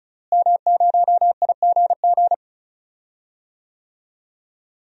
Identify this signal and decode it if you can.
Morse code, heard as a keyed tone, about 36 wpm, 700 Hz. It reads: M0IGG
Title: DXCC-100 Call Signs - 5 Characters - Repeated 2x - 35wpm